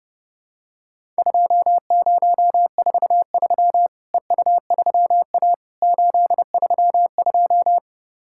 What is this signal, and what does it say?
Morse code, heard as a keyed tone, about 32 wpm, 700 Hz. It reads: 2043 EU3A 832
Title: Worked All Europe DX Contest - QTC Exchange - 30wpm